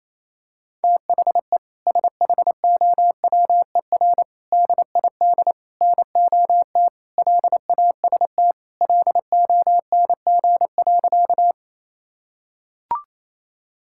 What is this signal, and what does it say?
Morse code, heard as a keyed tone, about 27 wpm, 700 Hz. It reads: THE SHOWER DID NOT LAST LONG.
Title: Sentences from Top 1600 Words 28wpm